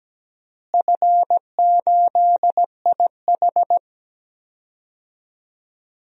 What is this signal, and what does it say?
Morse code, heard as a keyed tone, about 17 wpm, 700 Hz. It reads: F8IH
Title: DXCC-100 Call Signs - 4 Characters - Repeated 3x - 17wpm